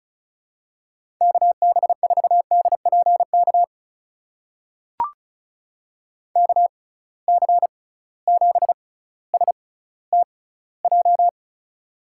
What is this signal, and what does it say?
Morse code, heard as a keyed tone, about 36 wpm, spaced as for 18 wpm, 700 Hz. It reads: KB4DPK KC7STJ
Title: USA Call Signs - 6 Characters - 6x Character Spacing - 35wpm